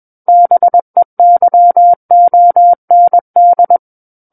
Morse code, heard as a keyed tone, about 21 wpm, 700 Hz. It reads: BEYOND
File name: Book One / 12-Chapter 12 - 21 WPM